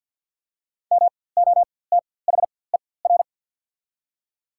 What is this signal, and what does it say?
Morse code, heard as a keyed tone, about 52 wpm, spaced as for 24 wpm, 700 Hz. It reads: MOTHER
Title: 4x Character Spacing - Top 500 Words - 50wpm